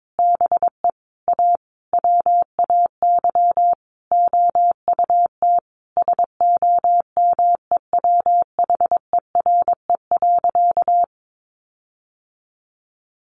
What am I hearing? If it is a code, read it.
Morse code, heard as a keyed tone, about 22 wpm, 700 Hz. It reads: BE A WAY OUT SOMEWHERE.
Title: Sentences from Top 1400 Words 22wpm